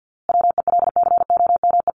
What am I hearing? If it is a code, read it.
Morse code, heard as a keyed tone, about 50 wpm, 700 Hz. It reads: WELCOME
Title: A Sound Of Thunder 50wpm